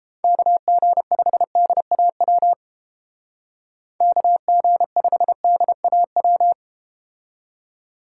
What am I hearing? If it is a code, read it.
Morse code, heard as a keyed tone, about 30 wpm, 700 Hz. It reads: KG5DAW KG5DAW